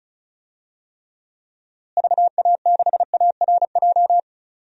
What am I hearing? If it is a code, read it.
Morse code, heard as a keyed tone, about 36 wpm, 700 Hz. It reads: VA6ARJ
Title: DXCC-100 Call Signs - 6 Characters - Repeated 2x - 35wpm